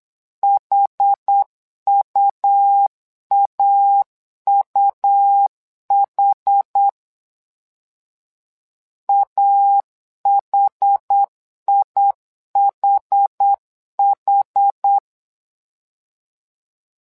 Morse code, heard as a keyed tone, about 8 wpm, 800 Hz. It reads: HUAUH AHIHH
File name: LEZIONI / 2 UHDN / 2 UH 8wpm 2129